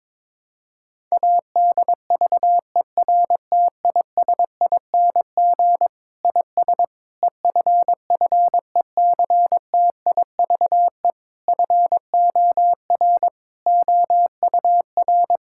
Morse code, heard as a keyed tone, about 22 wpm, 700 Hz. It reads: ADVERTISING IS EFFECTIVE FOR OUR
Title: Sentences from Top 1000 Words - Review Part 6 - 22wpm